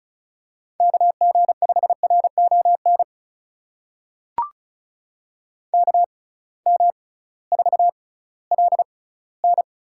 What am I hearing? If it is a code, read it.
Morse code, heard as a keyed tone, about 36 wpm, spaced as for 19 wpm, 700 Hz. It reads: KG5RON KM4LN
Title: USA Call Signs - 6 Characters - 6x Character Spacing - 35wpm